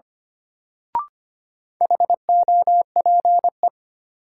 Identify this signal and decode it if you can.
Morse code, heard as a keyed tone, about 25 wpm, 700 Hz. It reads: HOPE
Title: Sentences from Top 1000 Words - New Word Drill - 25wpm